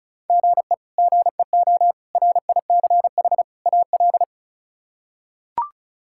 Morse code, heard as a keyed tone, about 36 wpm, 700 Hz. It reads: GE GEO RICH AL
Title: K1USN SST QSO Practice - 35wpm